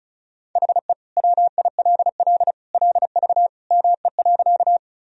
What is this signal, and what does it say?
Morse code, heard as a keyed tone, about 34 wpm, 700 Hz. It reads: HE WILL LV ME.
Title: Sentences from Top 400 Words and Abbreviations 35wpm